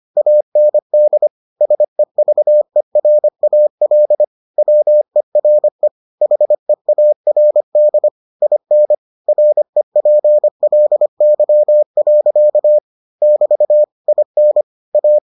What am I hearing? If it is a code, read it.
Morse code, heard as a keyed tone, about 25 wpm, 600 Hz. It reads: AND SEVERAL WERE HEARD IN REPLY. <BT> IN A